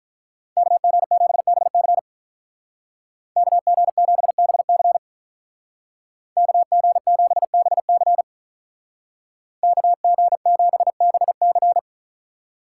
Morse code, heard as a keyed tone, about 44 wpm, 700 Hz. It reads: KG7BC KG7BC KG7BC KG7BC
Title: Speed Racing - USA 2x2 Call Signs 35wpm